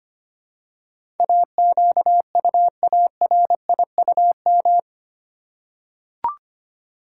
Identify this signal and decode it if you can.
Morse code, heard as a keyed tone, about 25 wpm, 700 Hz. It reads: AQUARIUM
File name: practice / 25wpm / Easy Words with Hints - 25wpm